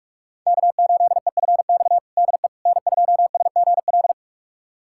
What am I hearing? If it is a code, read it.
Morse code, heard as a keyed tone, about 46 wpm, 700 Hz. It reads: K8EFX DE N2SGL
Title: USA Callsign DE Callsign - 5 Characters - 45wpm